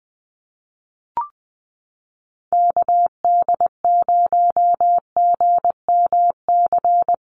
Morse code, heard as a keyed tone, about 20 wpm, 700 Hz. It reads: KD0GMC
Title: USA Call Signs Repeated 3x 20wpm